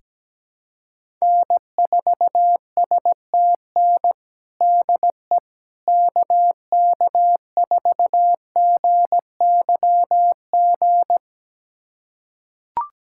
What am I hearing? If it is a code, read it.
Morse code, heard as a keyed tone, about 17 wpm, 700 Hz. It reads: N4STN DE KK4GYG
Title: USA Callsign DE Callsign 17wpm